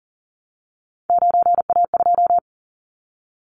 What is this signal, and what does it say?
Morse code, heard as a keyed tone, about 40 wpm, 700 Hz. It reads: 9A2
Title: Callsign Prefixes for Top 50 DXCC - 40wpm